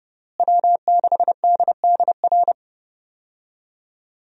Morse code, heard as a keyed tone, about 30 wpm, 700 Hz. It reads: W6DDR
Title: USA Call Signs - 5 Characters - Repeated 3x - 30wpm